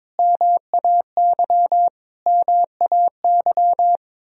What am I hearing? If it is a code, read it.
Morse code, heard as a keyed tone, about 21 wpm, 700 Hz. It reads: MAY MAY
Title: Top 100 Words Repeated 22wpm